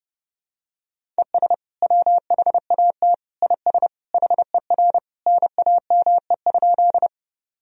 Morse code, heard as a keyed tone, about 30 wpm, 700 Hz. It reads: ES WHAT IS HER NAME?
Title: Sentences from Top 200 Words and Abbreviations 30wpm